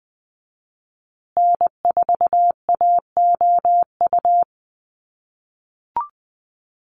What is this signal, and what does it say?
Morse code, heard as a keyed tone, about 20 wpm, 700 Hz. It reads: N4AOU
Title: USA Call Signs Repeated 2x 20wpm